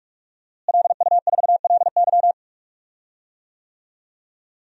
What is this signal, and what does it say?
Morse code, heard as a keyed tone, about 41 wpm, 700 Hz. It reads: LU4LY
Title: DXCC-100 Call Signs - 5 Characters - Repeated 3x - 45wpm